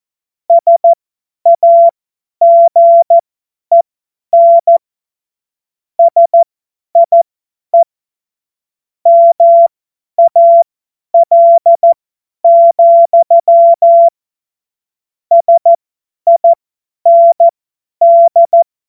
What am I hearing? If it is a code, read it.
Morse code, heard as a keyed tone, about 14 wpm, spaced as for 10 wpm, 680 Hz. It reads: SAGEN SIE MAL, SIND